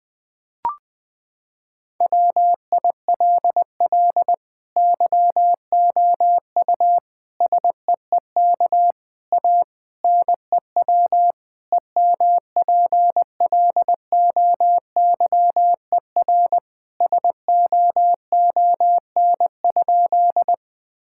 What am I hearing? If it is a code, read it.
Morse code, heard as a keyed tone, about 20 wpm, 700 Hz. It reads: WILL YOU SEEK A NEW EMPLOYER SOON?
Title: Sentences from Top 1600 Words 20wpm